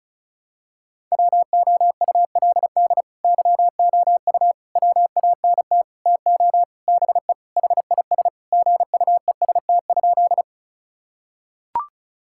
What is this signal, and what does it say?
Morse code, heard as a keyed tone, about 34 wpm, 700 Hz. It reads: WOULD YOU WANT TO BE HIS GUEST?
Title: Sentences from Top 1600 Words 35wpm